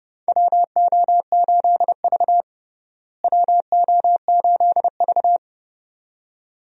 Morse code, heard as a keyed tone, about 30 wpm, 700 Hz. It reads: WO8V WO8V
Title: USA Call Signs - 4 Characters - Repeated 2x - 30wpm